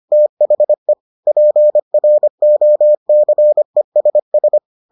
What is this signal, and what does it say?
Morse code, heard as a keyed tone, about 25 wpm, 600 Hz. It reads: THE PROCESS